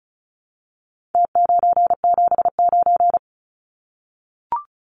Morse code, heard as a keyed tone, about 34 wpm, 700 Hz. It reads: T979
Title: Numbers - Contest Style - 4-Digit 35wpm